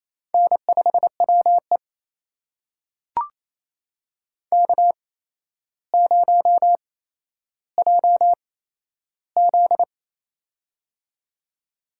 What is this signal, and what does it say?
Morse code, heard as a keyed tone, about 28 wpm, spaced as for 12 wpm, 700 Hz. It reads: N5WE K0JZ
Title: USA Call Signs - 4 Characters - 8x Character Spacing - 28wpm